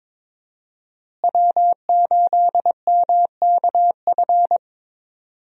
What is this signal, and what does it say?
Morse code, heard as a keyed tone, about 21 wpm, 700 Hz. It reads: W8MKF
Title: USA Call Signs - 5 Characters - Repeated 2x - 22wpm